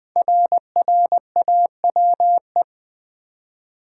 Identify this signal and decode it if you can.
Morse code, heard as a keyed tone, about 20 wpm, 700 Hz. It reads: RRAWE
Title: Code Groups - 5 Letters - 20wpm